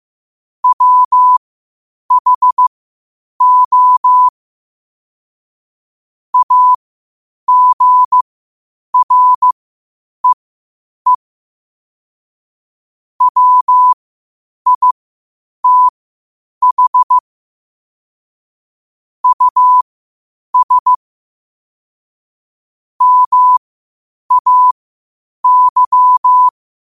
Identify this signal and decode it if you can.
Morse code, heard as a keyed tone, about 15 wpm, spaced as for 8 wpm, 1000 Hz. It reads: WHO AGREE WITH US MAY
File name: qotdfiles / 8wpm